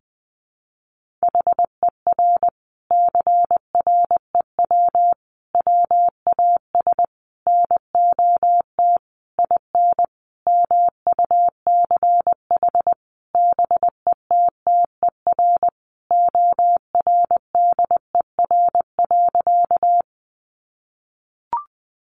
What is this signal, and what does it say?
Morse code, heard as a keyed tone, about 19 wpm, 700 Hz. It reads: HER CREW WAS NOT IN MUCH BETTER ORDER.
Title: Sentences from Top 1500 Words 20wpm